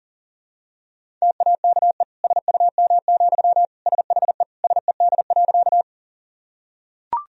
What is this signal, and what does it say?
Morse code, heard as a keyed tone, about 40 wpm, 700 Hz. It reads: TAKE SUM, SHE SED.
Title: Sentences from Top 200 Words and Abbreviations 40wpm